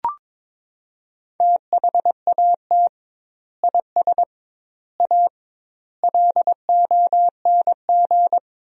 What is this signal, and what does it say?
Morse code, heard as a keyed tone, about 22 wpm, 700 Hz. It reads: THAT IS A LONG
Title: 2x Word-Spacing - Sentences from Top 100 Words - 22wpm